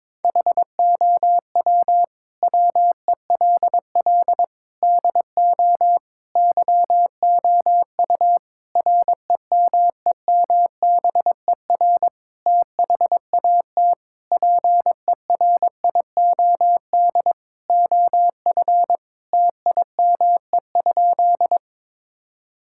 Morse code, heard as a keyed tone, about 22 wpm, 700 Hz. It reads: HOW WELL DO YOU REMEMBER THAT PERIOD OF TIME?
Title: Sentences from Top 1000 Words - Review Part 5 - 22wpm